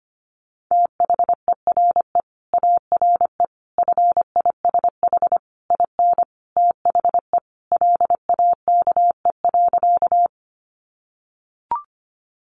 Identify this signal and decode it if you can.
Morse code, heard as a keyed tone, about 25 wpm, 700 Hz. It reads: THERE ARE FISH IN THE LAKE.
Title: Sentences from Top 1100 Words 25wpm